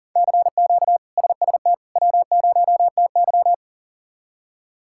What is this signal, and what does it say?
Morse code, heard as a keyed tone, about 37 wpm, 700 Hz. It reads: CQ SST W0TY
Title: K1USN SST QSO Practice - 40wpm